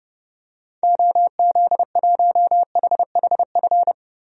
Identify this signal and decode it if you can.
Morse code, heard as a keyed tone, about 30 wpm, 700 Hz. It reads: OZ1HHF
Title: DXCC-100 Call Signs - 6 Characters - 30wpm